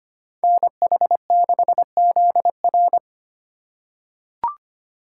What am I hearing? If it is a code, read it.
Morse code, heard as a keyed tone, about 25 wpm, 700 Hz. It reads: NH6ZR
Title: USA Call Signs - 5 Characters - 2x Character Spacing - 25wpm